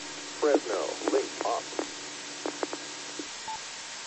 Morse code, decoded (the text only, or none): none